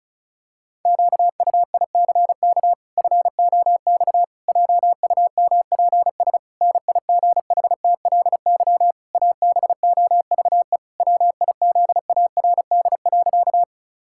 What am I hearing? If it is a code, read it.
Morse code, heard as a keyed tone, about 36 wpm, 700 Hz. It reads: QUICK FOX JUMPS NIGHTLY ABOVE WIZARD.
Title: Pangram Challenge v2 - 35wpm